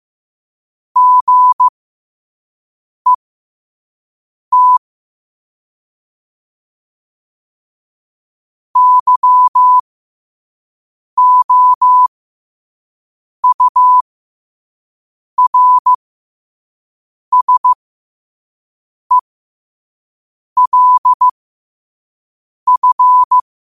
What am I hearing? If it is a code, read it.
Morse code, heard as a keyed tone, about 15 wpm, spaced as for 5 wpm, 1000 Hz. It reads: GET YOURSELF
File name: qotdfiles / 5wpm